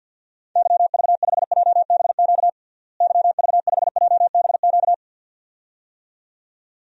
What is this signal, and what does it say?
Morse code, heard as a keyed tone, about 50 wpm, 700 Hz. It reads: YV5JBQ YV5JBQ
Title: DXCC-100 Call Signs - 6 Characters - Repeated 2x - 50wpm